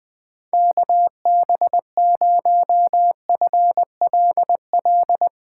Morse code, heard as a keyed tone, about 20 wpm, 700 Hz. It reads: KB0FLL
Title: Speed Racing - USA 2x3 Call Signs 17wpm